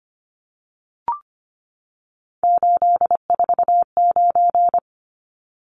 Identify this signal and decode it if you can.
Morse code, heard as a keyed tone, about 25 wpm, 700 Hz. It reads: 849
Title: Numbers - 3-Digit 25wpm